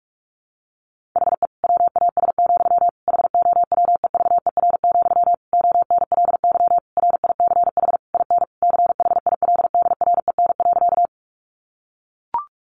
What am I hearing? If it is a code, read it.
Morse code, heard as a keyed tone, about 44 wpm, 700 Hz. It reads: HE WAS, HOWEVER, ONLY RICH IN CHILDREN.
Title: Sentences from Top 1100 Words 45wpm